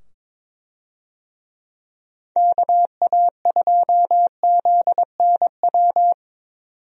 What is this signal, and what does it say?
Morse code, heard as a keyed tone, about 21 wpm, 700 Hz. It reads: KA2ZNW
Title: USA Call Signs - 6 Characters - 22wpm